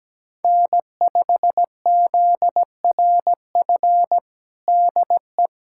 Morse code, heard as a keyed tone, about 17 wpm, 700 Hz. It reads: N5ZRF DE
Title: USA Callsign DE Callsign - 5 Characters - 17wpm